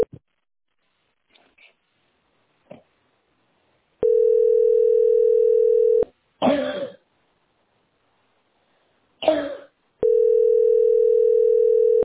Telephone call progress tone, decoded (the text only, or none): ringback tone